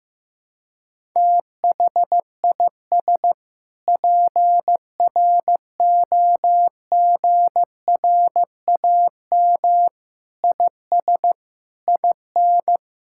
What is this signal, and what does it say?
Morse code, heard as a keyed tone, about 15 wpm, 700 Hz. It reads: THIS PROGRAM IS IN
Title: Sentences from Top 1000 Words - Review Part 5 - 15wpm